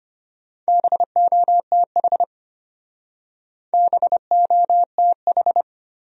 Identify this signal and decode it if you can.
Morse code, heard as a keyed tone, about 32 wpm, 700 Hz. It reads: BOTH BOTH